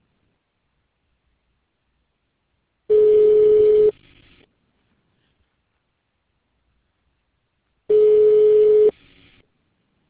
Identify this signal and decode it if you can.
Ringback tone (Europe, 425 Hz).